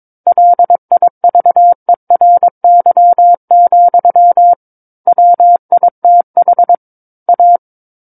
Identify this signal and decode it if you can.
Morse code, heard as a keyed tone, about 22 wpm, 700 Hz. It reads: LIVERY, WITH A
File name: 06-Chapter-06 22 WPM